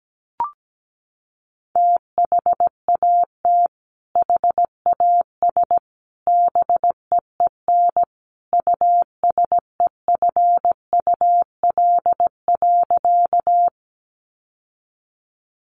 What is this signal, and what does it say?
Morse code, heard as a keyed tone, about 17 wpm, 700 Hz. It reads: THAT HAS BEEN USEFUL.
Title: Sentences from Top 700 Words 17wpm